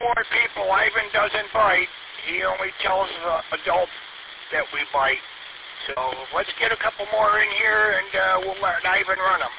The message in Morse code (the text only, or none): none